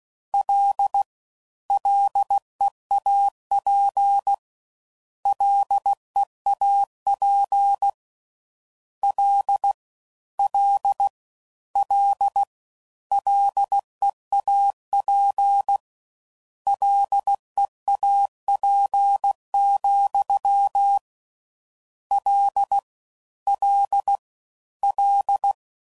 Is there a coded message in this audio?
Yes: Morse code from 0.3 to 25.5 s, heard as a keyed tone, about 16 wpm, 800 Hz. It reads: L LEAP LEAP L L L LEAP LEAP, L L L